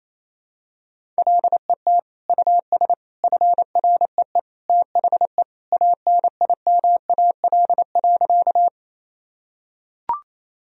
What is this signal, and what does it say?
Morse code, heard as a keyed tone, about 28 wpm, 700 Hz. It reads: LET US FREE THE ANIMAL.
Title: Sentences from Top 1000 Words - Review Part 1 - 28wpm